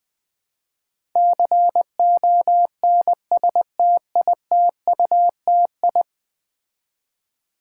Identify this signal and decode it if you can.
Morse code, heard as a keyed tone, about 20 wpm, 700 Hz. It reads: CONSTITUTI
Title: Build A Word - 5 Letters At A Time - 12-Letter Words - 20wpm